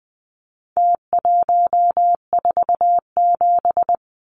Morse code, heard as a keyed tone, about 20 wpm, 700 Hz. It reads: T147
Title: Numbers - Contest Style - 4-Digit 20wpm